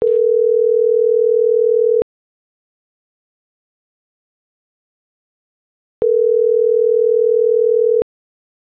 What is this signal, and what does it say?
Ringback tone (North America), 440 + 480 Hz.